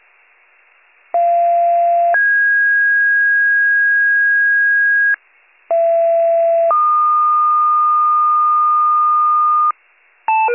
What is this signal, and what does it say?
Two-tone page, heard as a one-second tone, then a three-second tone, about 690 then 1650 Hz, about 670 then 1220 Hz, with noise. It is heard from 1.1 to 9.7 s.